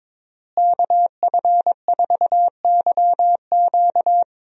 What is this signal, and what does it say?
Morse code, heard as a keyed tone, about 22 wpm, 700 Hz. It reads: KF4YQ
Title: USA Call Signs - 5 Characters - Repeated 2x - 22wpm